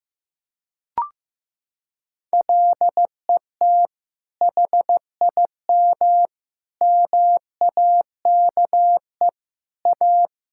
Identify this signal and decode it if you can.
Morse code, heard as a keyed tone, about 15 wpm, 700 Hz. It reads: LET HIM MAKE A
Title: Sentences from Top 1000 Words - Review Part 6 - 15wpm